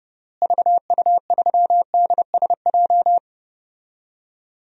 Morse code, heard as a keyed tone, about 32 wpm, 700 Hz. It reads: VU3DSJ
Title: DXCC-100 Call Signs - 6 Characters - Repeated 3x - 30wpm